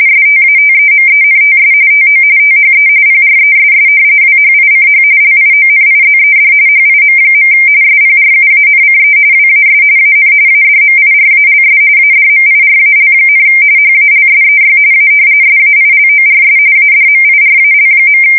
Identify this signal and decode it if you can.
RTTY, about 45.45 baud. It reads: STATEMENTS TO THE PRESS WHICH LIONIZED HIM. SPRUANCE WAS AN ADMIRALS ADMIRAL, HALSEY A SAILORS ADMIRAL.
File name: fsk170 / battleshipnj